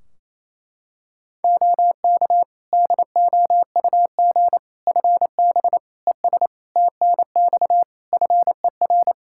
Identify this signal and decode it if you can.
Morse code, heard as a keyed tone, about 28 wpm, 700 Hz. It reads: OK DOUG FB ES TNX FER
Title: Practice QSO - Exchange 2 - Info - 28wpm